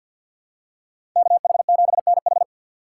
Morse code, heard as a keyed tone, about 50 wpm, 700 Hz. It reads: KH7NH
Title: USA Call Signs - 2x Character Spacing - 50wpm